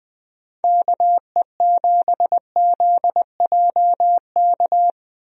Morse code, heard as a keyed tone, about 20 wpm, 700 Hz. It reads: KE7ZJK
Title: USA Call Signs - 6 Characters - Repeated 2x - 20wpm